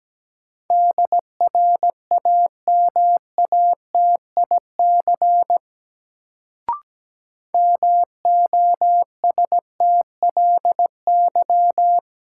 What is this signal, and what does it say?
Morse code, heard as a keyed tone, about 17 wpm, 700 Hz. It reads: DRAMATIC MOSTLY
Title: Sentences from Top 1300 Words - New Word Drill Alt - 17wpm